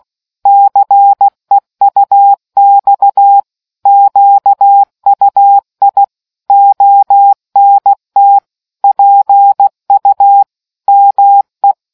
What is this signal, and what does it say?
Morse code, heard as a keyed tone, about 16 wpm, 800 Hz. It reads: CEUX QUI ONT PU ME